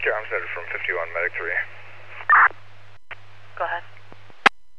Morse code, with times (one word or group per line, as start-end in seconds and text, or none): none